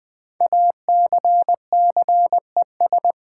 Morse code, heard as a keyed tone, about 20 wpm, 700 Hz. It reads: ACCES
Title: Build A Word - 5 Letters At A Time - 10-Letter Words - 20wpm